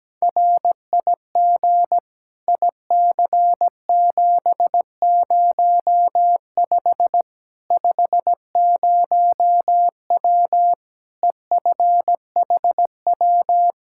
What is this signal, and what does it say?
Morse code, heard as a keyed tone, about 17 wpm, 700 Hz. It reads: RIG IC705 50W EFHW
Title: Practice QSO - Exchange 2 - Info - 17wpm